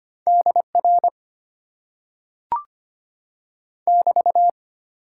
Morse code, heard as a keyed tone, about 25 wpm, 700 Hz. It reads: DR <BT>
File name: Top 90 QSO Elements - 8x Character Spacing - 25wpm